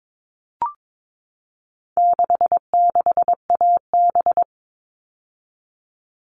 Morse code, heard as a keyed tone, about 22 wpm, 700 Hz. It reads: 66AB
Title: All Cut Numbers - 4-Digit 22wpm